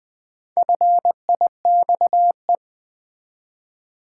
Morse code, heard as a keyed tone, about 20 wpm, 700 Hz. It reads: FIXE
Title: Build A Word - 1 Letter At A Time - 5-Letter Words - 20wpm